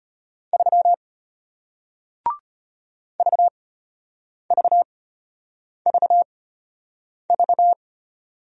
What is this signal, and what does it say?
Morse code, heard as a keyed tone, about 33 wpm, spaced as for 8 wpm, 700 Hz. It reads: 3 VVVV